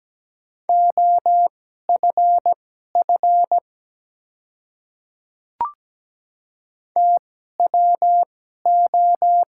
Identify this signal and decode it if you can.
Morse code, heard as a keyed tone, about 17 wpm, spaced as for 12 wpm, 700 Hz. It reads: OFF TWO
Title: 2x Character Spacing - Top 200 Words - 17wpm